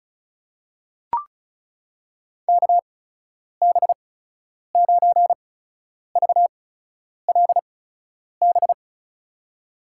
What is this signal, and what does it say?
Morse code, heard as a keyed tone, about 34 wpm, spaced as for 10 wpm, 700 Hz. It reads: KB9VLB
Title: USA Call Signs - 6 Characters - 8x Character Spacing - 35wpm